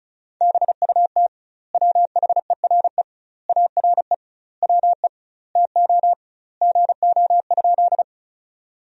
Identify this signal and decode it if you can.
Morse code, heard as a keyed tone, about 36 wpm, 700 Hz. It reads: BUT WHERE ARE WE TO GO?
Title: 2x Word-Spacing - Sentences from Top 200 Words - 35wpm